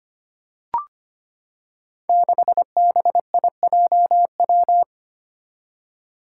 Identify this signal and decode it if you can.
Morse code, heard as a keyed tone, about 25 wpm, 700 Hz. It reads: 6BIJW
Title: Five Challenging Letter-Number Combinations 25wpm